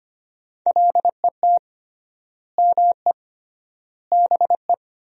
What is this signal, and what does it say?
Morse code, heard as a keyed tone, about 25 wpm, 700 Hz. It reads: LET ME BE
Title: 3x Word-Spacing - Sentences from Top 300 Words - 25wpm